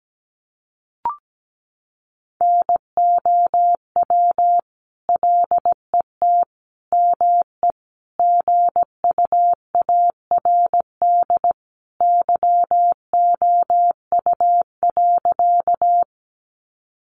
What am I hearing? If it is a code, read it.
Morse code, heard as a keyed tone, about 17 wpm, 700 Hz. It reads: NOW LET ME GUARD YOU.
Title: Sentences from Top 700 Words 17wpm